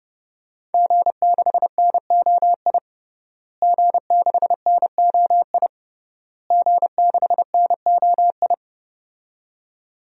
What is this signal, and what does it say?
Morse code, heard as a keyed tone, about 30 wpm, 700 Hz. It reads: G6NOI G6NOI G6NOI
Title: DXCC-100 Call Signs - 5 Characters - Repeated 3x - 30wpm